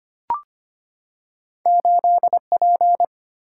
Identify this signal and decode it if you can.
Morse code, heard as a keyed tone, about 25 wpm, 700 Hz. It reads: 8P
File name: Two Letter-Number Combinations - 25wpm